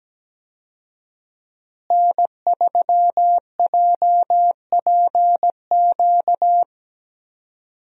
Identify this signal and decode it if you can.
Morse code, heard as a keyed tone, about 17 wpm, 700 Hz. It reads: N3JPQ
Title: USA Call Signs - 5 Characters - Repeated 2x - 17wpm